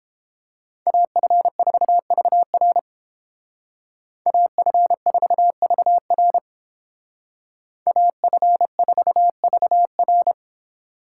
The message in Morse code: AF4VR AF4VR AF4VR

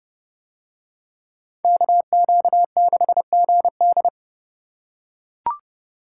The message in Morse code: KQ6GD

